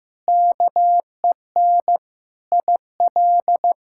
Morse code, heard as a keyed tone, about 15 wpm, 700 Hz. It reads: KEN IL
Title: K1USN SST QSO Practice - 15wpm